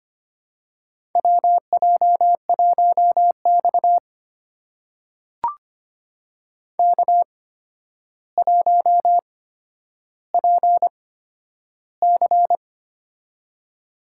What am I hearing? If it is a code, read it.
Morse code, heard as a keyed tone, about 25 wpm, spaced as for 11 wpm, 700 Hz. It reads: WJ1X K1PC
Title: USA Call Signs - 4 Characters - 8x Character Spacing - 25wpm